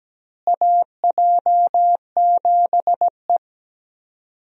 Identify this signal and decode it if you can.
Morse code, heard as a keyed tone, about 17 wpm, 700 Hz. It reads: AJ7E
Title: USA Call Signs - 4 Characters - Repeated 2x - 17wpm